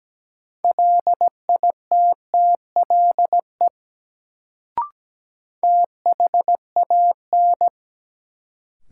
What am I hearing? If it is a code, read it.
Morse code, heard as a keyed tone, about 17 wpm, 700 Hz. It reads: LITTLE THAN
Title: Sets of 2 Words - New Word Drill - 17wpm